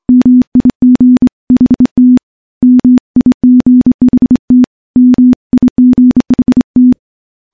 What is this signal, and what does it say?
Morse code, heard as a keyed tone, about 25 wpm, 250 Hz. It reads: MIGHT MIGHT MIGHT
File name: morse-training-25WPM